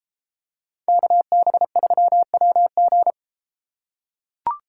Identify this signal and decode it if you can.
Morse code, heard as a keyed tone, about 32 wpm, 700 Hz. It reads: KB3WG